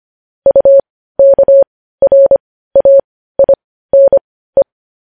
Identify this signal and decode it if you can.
Morse code, heard as a keyed tone, about 25 wpm, spaced as for 15 wpm, 550 Hz. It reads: UKRAINE
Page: news headlines in morse code at 15 wpm